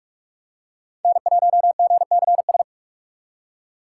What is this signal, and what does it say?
Morse code, heard as a keyed tone, about 46 wpm, 700 Hz. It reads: N1GCS
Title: USA Call Signs - 3x Character Spacing - 45wpm